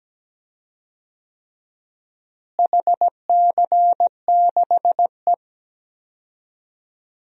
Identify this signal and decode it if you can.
Morse code, heard as a keyed tone, about 17 wpm, 700 Hz. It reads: HC6E
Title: DXCC-100 Call Signs - 4 Characters - Repeated 3x - 17wpm